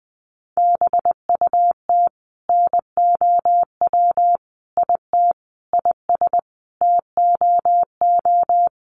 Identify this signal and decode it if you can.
Morse code, heard as a keyed tone, about 20 wpm, 700 Hz. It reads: BUT NOW IT IS TOO
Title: Sentences from Top 300 Words 20wpm